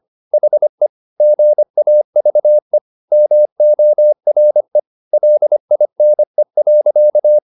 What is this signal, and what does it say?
Morse code, heard as a keyed tone, about 25 wpm, 600 Hz. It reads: HE GAVE MORE LINE.
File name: the old man and the sea part 1 25wpm